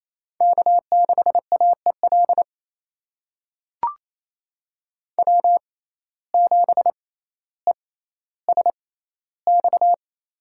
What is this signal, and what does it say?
Morse code, heard as a keyed tone, about 28 wpm, spaced as for 14 wpm, 700 Hz. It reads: K6AEL W7ESX
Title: USA Call Signs - 5 Characters - 6x Character Spacing - 28wpm